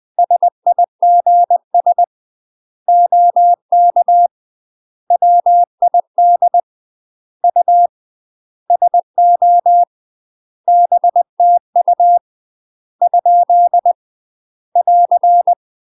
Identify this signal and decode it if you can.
Morse code, heard as a keyed tone, about 20 wpm, 700 Hz. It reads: SIGS OK WID U SO BTU ? <AR>
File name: UK MASTER QSOs TRAINING/20WPM